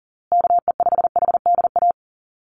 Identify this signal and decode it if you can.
Morse code, heard as a keyed tone, about 40 wpm, 700 Hz. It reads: KE5HDA
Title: USA Call Signs Repeated 2x 40wpm